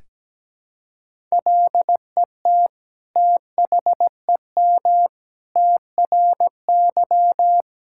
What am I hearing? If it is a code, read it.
Morse code, heard as a keyed tone, about 17 wpm, 700 Hz. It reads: LET THEM TRY
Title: Sentences from Top 1000 Words - Review Part 5 - 17wpm